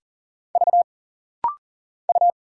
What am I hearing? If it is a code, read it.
Morse code, heard as a keyed tone, about 40 wpm, 700 Hz. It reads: V U
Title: Single Letter-Number Alt - Rapid-Fire 40wpm